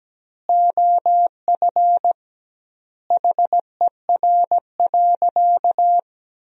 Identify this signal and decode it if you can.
Morse code, heard as a keyed tone, about 17 wpm, 700 Hz. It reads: OF HER.
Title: 2x Word-Spacing - Sentences from Top 200 Words - 17wpm